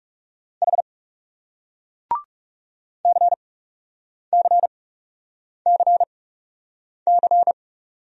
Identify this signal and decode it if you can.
Morse code, heard as a keyed tone, about 41 wpm, spaced as for 8 wpm, 700 Hz. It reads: H CCCC